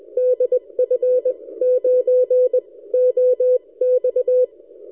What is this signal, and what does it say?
Morse code, heard as a keyed tone, about 21 wpm, spaced as for 18 wpm, 520 Hz. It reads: DF9OX
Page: during this recording I switched the rx loop direction front to the backside K3ZM almost disappeared so my K9AY loop seems to work okay recorded amazing 160m signal